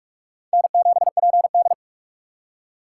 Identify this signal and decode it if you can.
Morse code, heard as a keyed tone, about 44 wpm, 700 Hz. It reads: N7PD